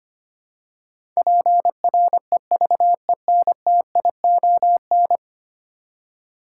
Morse code, heard as a keyed tone, about 25 wpm, 700 Hz. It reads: PREVENTION
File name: Ten Letter Words - 25wpm